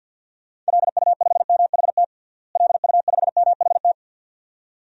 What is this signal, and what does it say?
Morse code, heard as a keyed tone, about 51 wpm, 700 Hz. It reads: LU5MHT LU5MHT
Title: DXCC-100 Call Signs - 6 Characters - Repeated 2x - 50wpm